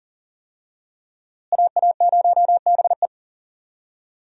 Morse code, heard as a keyed tone, about 39 wpm, 700 Hz. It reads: AA0BE
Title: USA Call Signs - 5 Characters - Repeated 3x - 40wpm